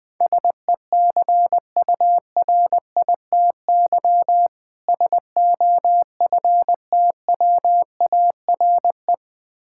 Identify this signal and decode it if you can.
Morse code, heard as a keyed tone, about 20 wpm, 700 Hz. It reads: SECURITY SOFTWARE
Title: Difficult Words - Speed-Racing with Hints - 20wpm